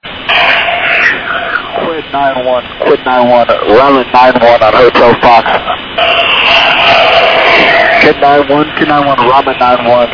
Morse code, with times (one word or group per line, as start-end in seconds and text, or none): none